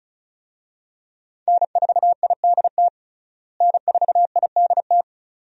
Morse code, heard as a keyed tone, about 36 wpm, 700 Hz. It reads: N4IDT N4IDT
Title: USA Call Signs - 5 Characters - Repeated 3x - 35wpm